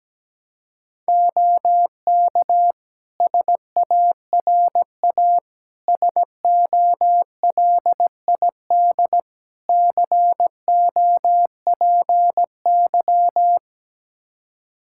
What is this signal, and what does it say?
Morse code, heard as a keyed tone, about 17 wpm, 700 Hz. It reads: OK SARA SOLID COPY
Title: Practice QSO - Exchange 3 - More Info - 17wpm